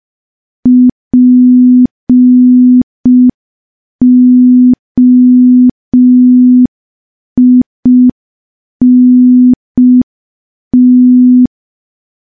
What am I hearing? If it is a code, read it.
Morse code, heard as a keyed tone, about 5 wpm, 250 Hz. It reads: POINT
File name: morse / morse-training-05WPM